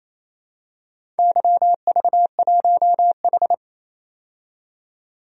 Morse code, heard as a keyed tone, about 29 wpm, 700 Hz. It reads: YV1H
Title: DXCC-100 Call Signs - 4 Characters - 28wpm